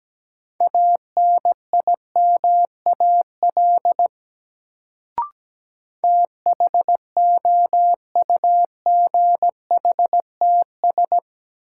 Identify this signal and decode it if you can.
Morse code, heard as a keyed tone, about 17 wpm, 700 Hz. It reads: ANIMAL THOUGHTS
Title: Sentences from Top 1000 Words - Review Part 1 - New Word Drill - 17wpm